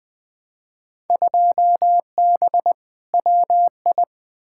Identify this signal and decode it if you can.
Morse code, heard as a keyed tone, about 20 wpm, 700 Hz. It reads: 2B WI
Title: Field Day Exchanges - 20wpm